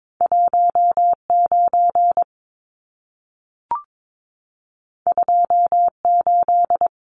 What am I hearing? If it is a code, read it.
Morse code, heard as a keyed tone, about 22 wpm, 700 Hz. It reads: 19 28
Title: Numbers - 2-Digit 22wpm